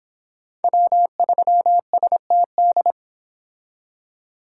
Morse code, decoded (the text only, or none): W3STD